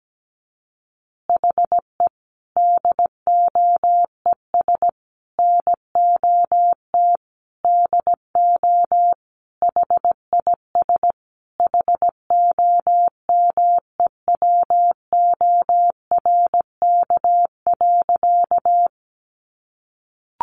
Morse code, heard as a keyed tone, about 17 wpm, 700 Hz. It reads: HE DOES NOT DO HIS HOMEWORK.
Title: Additional Easy Phrases 17wpm